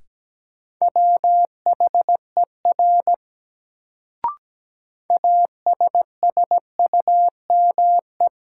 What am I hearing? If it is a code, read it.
Morse code, heard as a keyed tone, about 17 wpm, 700 Hz. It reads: WHER ASSUME
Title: Sentences from Top 1500 Words - New Word Drill Alt - 17wpm